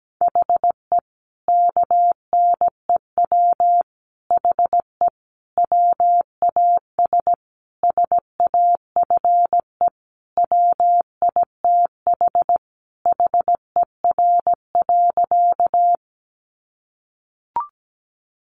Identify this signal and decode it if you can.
Morse code, heard as a keyed tone, about 17 wpm, 700 Hz. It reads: HE KNEW HE WAS SAFE WITH HER.
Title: Sentences from Top 900 Words 17wpm